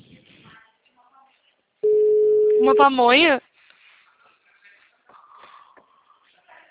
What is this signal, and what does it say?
Ringback tone (Europe, 425 Hz), with speech.
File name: out- -2012- - - .38716